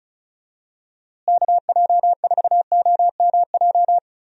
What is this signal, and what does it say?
Morse code, heard as a keyed tone, about 34 wpm, 700 Hz. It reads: KJ4OMJ
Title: USA Call Signs - 6 Characters - Repeated 2x - 35wpm